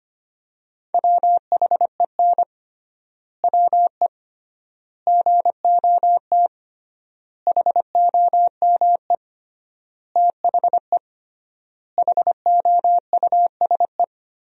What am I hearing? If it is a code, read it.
Morse code, heard as a keyed tone, about 25 wpm, 700 Hz. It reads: WHEN WE GOT HOME THE HOUSE
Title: 3x Word-Spacing - Sentences from Top 500 Words - 25wpm